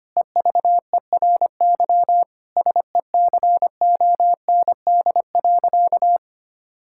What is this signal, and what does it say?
Morse code, heard as a keyed tone, about 25 wpm, 700 Hz. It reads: EVERY SECOND.
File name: practice / 25wpm / Difficult Sentences with Hints - 25wpm